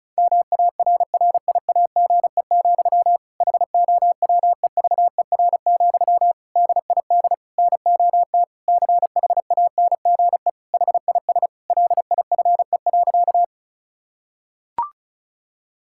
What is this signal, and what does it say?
Morse code, heard as a keyed tone, about 34 wpm, 700 Hz. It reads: MARRIAGE, HOWEVER, DID NOT CHANGE HIS LIFE.
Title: Sentences from Top 1000 Words - Review Part 5 - 35wpm